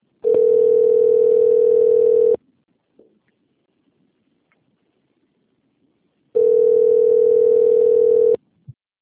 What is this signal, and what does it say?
Ringback tone (North America), 440 + 480 Hz.
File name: rg-1600-unknown- - - .54704